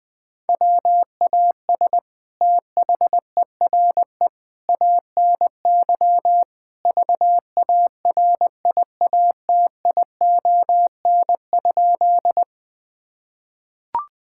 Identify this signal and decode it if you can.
Morse code, heard as a keyed tone, about 21 wpm, 700 Hz. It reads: WAS THERE ANY VARIATION?
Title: Sentences from Top 1600 Words 20wpm